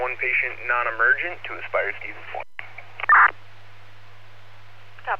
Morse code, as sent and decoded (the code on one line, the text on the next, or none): none